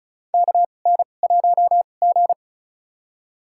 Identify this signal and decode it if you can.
Morse code, heard as a keyed tone, about 36 wpm, spaced as for 25 wpm, 700 Hz. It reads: KN1G
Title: USA Call Signs - 2x Character Spacing - 35wpm